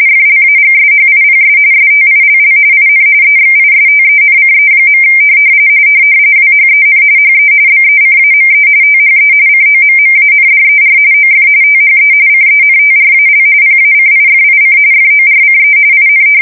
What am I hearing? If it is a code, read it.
RTTY, about 45.45 baud. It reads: FLAGSHIPS STILL IN EXISTENCE. WHEN HE BECAME THE COMMANDER OF THE CENTRAL PACIFIC FORCE IN
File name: fsk170 / battleshipnj